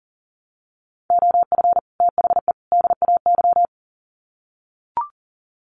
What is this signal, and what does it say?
Morse code, heard as a keyed tone, about 40 wpm, 700 Hz. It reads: OF THE DAY
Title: Sets of 3 Words 40wpm